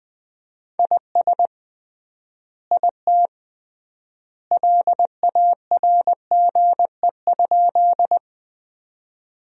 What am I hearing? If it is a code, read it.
Morse code, heard as a keyed tone, about 20 wpm, 700 Hz. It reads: IS IT LARGE?
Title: 3x Word-Spacing - Sentences from Top 200 Words - 20wpm